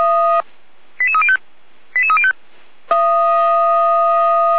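Five-tone selective call (ZVEI), heard as quick runs of five tones, tones of about 70 ms. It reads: 88305 88305
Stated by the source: Five-tone ZVEI selective call received on an SDR.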